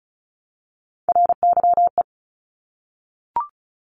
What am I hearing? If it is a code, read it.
Morse code, heard as a keyed tone, about 36 wpm, 700 Hz. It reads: RYE
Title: Multi-letter Phonemes Repeated 3x 35wpm